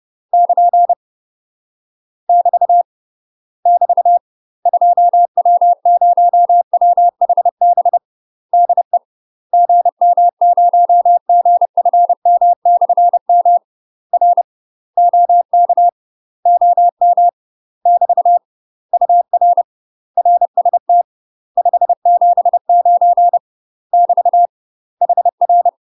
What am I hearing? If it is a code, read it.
Morse code, heard as a keyed tone, about 30 wpm, 700 Hz. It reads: <KN> <BT> <BT> 2W0WHB DE GM0GFM/M R OK OM <BT> UR RST 579 <BT> HR
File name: UK MASTER QSOs TRAINING / 30WPM / UK MASTER QSOs 0049